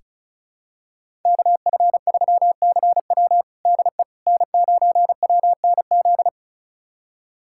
Morse code, heard as a keyed tone, about 36 wpm, 700 Hz. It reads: KF3CW DE N9WNZ
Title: USA Callsign DE Callsign - 5 Characters - 35wpm